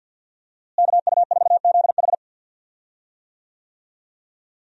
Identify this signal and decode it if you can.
Morse code, heard as a keyed tone, about 46 wpm, 700 Hz. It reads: KU4ZH
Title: USA Call Signs - 5 Characters - Repeated 3x - 50wpm